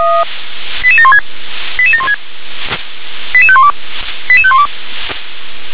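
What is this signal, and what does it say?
Five-tone selective call (ZVEI), heard as quick runs of five tones, tones of about 70 ms. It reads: 88716 88716 88512 88512
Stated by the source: ZVEI five-tone call received by SDR.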